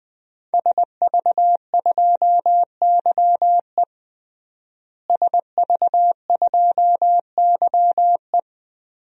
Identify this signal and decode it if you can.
Morse code, heard as a keyed tone, about 20 wpm, 700 Hz. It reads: SV2YE SV2YE
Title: DXCC-100 Call Signs - 5 Characters - Repeated 3x - 20wpm